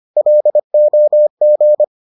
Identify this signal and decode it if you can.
Morse code, heard as a keyed tone, about 25 wpm, 600 Hz. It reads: LOG